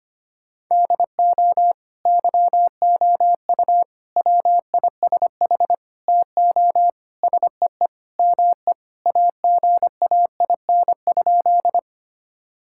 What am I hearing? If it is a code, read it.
Morse code, heard as a keyed tone, about 25 wpm, 700 Hz. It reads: DO YOU WISH TO SEE ME AGAIN?
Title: Sentences from Top 1600 Words 25wpm